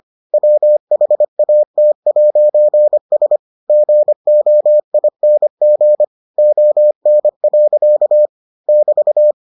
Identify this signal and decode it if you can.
Morse code, heard as a keyed tone, about 25 wpm, 600 Hz. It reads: WHAT'S GOING ON. <BT>